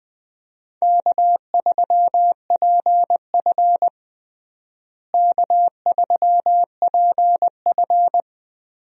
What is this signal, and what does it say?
Morse code, heard as a keyed tone, about 21 wpm, 700 Hz. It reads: K3PF K3PF
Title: USA Call Signs - 4 Characters - Repeated 2x - 20wpm